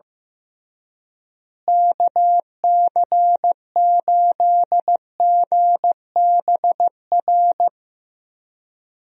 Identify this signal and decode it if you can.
Morse code, heard as a keyed tone, about 15 wpm, 700 Hz. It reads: KC8GBR
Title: USA Call Signs - 6 Characters - Repeated 3x - 15wpm